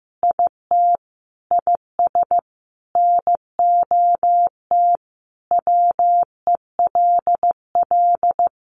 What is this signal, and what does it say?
Morse code, heard as a keyed tone, about 15 wpm, 700 Hz. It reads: IT IS NOT WELL
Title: Sentences from Top 1200 Words 15wpm